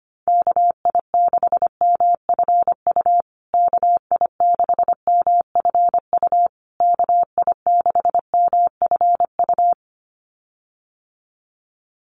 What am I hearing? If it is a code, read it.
Morse code, heard as a keyed tone, about 25 wpm, 700 Hz. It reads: KI6MFU KI6MFU KI6MFU
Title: USA Call Signs Repeated 3x 25wpm